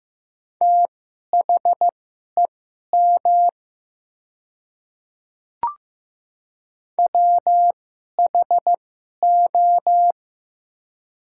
Morse code, heard as a keyed tone, about 15 wpm, spaced as for 11 wpm, 700 Hz. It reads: THEM WHO
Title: 2x Character Spacing - Top 100 Words - 15wpm